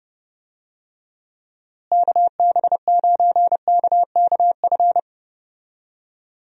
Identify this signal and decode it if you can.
Morse code, heard as a keyed tone, about 29 wpm, 700 Hz. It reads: KB9KKF
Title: USA Call Signs - 6 Characters - Repeated 2x - 30wpm